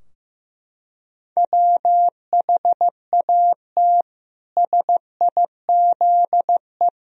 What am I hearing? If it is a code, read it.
Morse code, heard as a keyed tone, about 15 wpm, 700 Hz. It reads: WHAT SIZE
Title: Sentences from Top 1000 Words - Review Part 2 - 15wpm